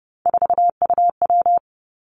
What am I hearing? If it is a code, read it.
Morse code, heard as a keyed tone, about 32 wpm, 700 Hz. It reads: 4UW
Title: Three Challenging Letter-Number Combinations 30wpm